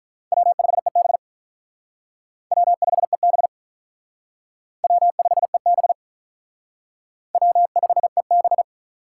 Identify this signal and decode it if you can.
Morse code, heard as a keyed tone, about 44 wpm, 700 Hz. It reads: W5EB W5EB W5EB W5EB
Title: Speed Racing - USA Call Signs 35wpm